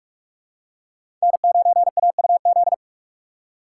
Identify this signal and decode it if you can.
Morse code, heard as a keyed tone, about 46 wpm, 700 Hz. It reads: N9AUZ
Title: USA Call Signs - 5 Characters - 6x Character Spacing - 45wpm